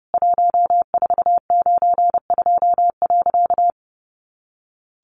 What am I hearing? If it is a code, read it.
Morse code, heard as a keyed tone, about 30 wpm, 700 Hz. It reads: 1492.
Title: A Sound Of Thunder 30wpm